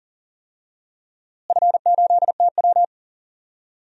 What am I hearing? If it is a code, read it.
Morse code, heard as a keyed tone, about 37 wpm, 700 Hz. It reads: F8TW
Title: DXCC-100 Call Signs - 4 Characters - Repeated 2x - 40wpm